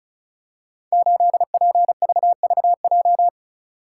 Morse code, heard as a keyed tone, about 36 wpm, 700 Hz. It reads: 8PVVJ
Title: Code Groups - 5 Letter:Numbers - 35wpm